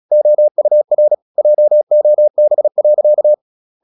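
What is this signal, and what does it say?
Morse code, heard as a keyed tone, about 36 wpm, 600 Hz. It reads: OUR JOB.
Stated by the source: Morse practice at 36 wpm, text of Ken Follett's novel Never, from CW med Gnister.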